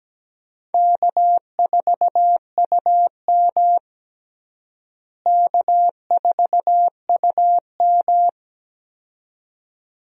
Morse code, heard as a keyed tone, about 17 wpm, 700 Hz. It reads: K4UM K4UM
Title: USA Call Signs - 4 Characters - Repeated 2x - 17wpm